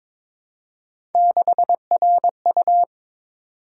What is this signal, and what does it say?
Morse code, heard as a keyed tone, about 21 wpm, 700 Hz. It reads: 6RU
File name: practice / 22wpm / Three Letter-Number Combinations - 22wpm